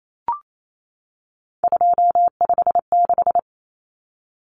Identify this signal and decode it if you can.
Morse code, heard as a keyed tone, about 28 wpm, 700 Hz. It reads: 256
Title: Common Cut Numbers - 3-Digit 28wpm